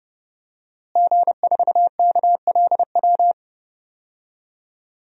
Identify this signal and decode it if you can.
Morse code, heard as a keyed tone, about 30 wpm, 700 Hz. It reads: G4KLW
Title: DXCC-100 Call Signs - 5 Characters - Repeated 3x - 30wpm